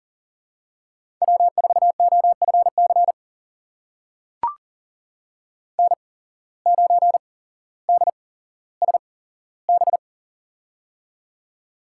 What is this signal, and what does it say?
Morse code, heard as a keyed tone, about 39 wpm, spaced as for 17 wpm, 700 Hz. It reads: W4OFC N9DSB
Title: USA Call Signs - 5 Characters - 8x Character Spacing - 40wpm